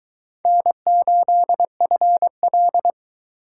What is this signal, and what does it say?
Morse code, heard as a keyed tone, about 22 wpm, 700 Hz. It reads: N8FL